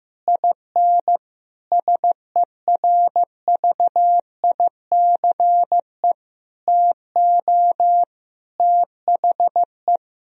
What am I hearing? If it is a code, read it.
Morse code, heard as a keyed tone, about 15 wpm, 700 Hz. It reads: IN SERVICE TO THE
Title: Sentences from Top 1000 Words - Review Part 4 - 15wpm